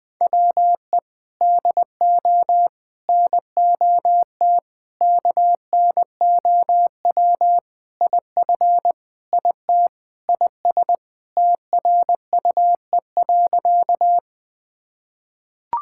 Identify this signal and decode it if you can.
Morse code, heard as a keyed tone, about 21 wpm, 690 Hz. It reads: WE DO NOT KNOW IF IT IS TRUE.
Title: Sentences with 4-Word Phrases - Spoken Beginning - 20wpm